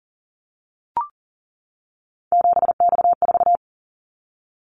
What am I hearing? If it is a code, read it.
Morse code, heard as a keyed tone, about 39 wpm, 700 Hz. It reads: 7X4
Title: Callsign Prefixes for Top 100 DXCC - 40wpm